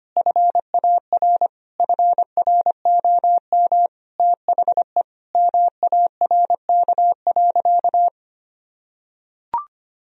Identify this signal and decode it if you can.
Morse code, heard as a keyed tone, about 25 wpm, 700 Hz. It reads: FAR FROM THE MARK.
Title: Sentences from Top 1000 Words - Review Part 1 - 25wpm